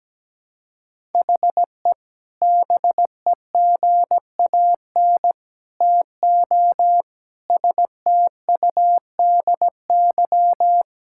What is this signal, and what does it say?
Morse code, heard as a keyed tone, about 17 wpm, 700 Hz. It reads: HE BEGAN TO STUDY
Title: Sentences from Top 1000 Words - Review Part 6 - 17wpm